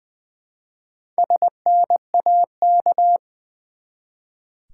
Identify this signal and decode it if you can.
Morse code, heard as a keyed tone, about 20 wpm, 700 Hz. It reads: SNAK
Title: Build A Word - 1 Letter At A Time - 5-Letter Words - 20wpm